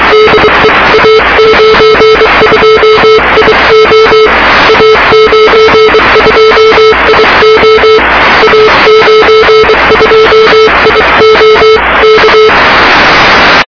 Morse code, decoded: DE A92IO A92IO A92IO K